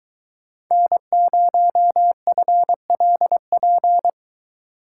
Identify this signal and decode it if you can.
Morse code, heard as a keyed tone, about 24 wpm, 700 Hz. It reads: N0FLP